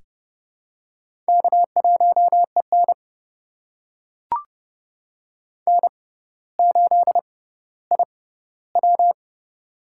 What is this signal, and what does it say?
Morse code, heard as a keyed tone, about 30 wpm, spaced as for 15 wpm, 700 Hz. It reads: K1EN N8IW
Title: USA Call Signs - 4 Characters - 6x Character Spacing - 30wpm